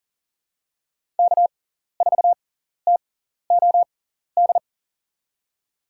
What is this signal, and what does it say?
Morse code, heard as a keyed tone, about 40 wpm, spaced as for 14 wpm, 700 Hz. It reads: K4TOD
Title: USA Call Signs - 5 Characters - 6x Character Spacing - 40wpm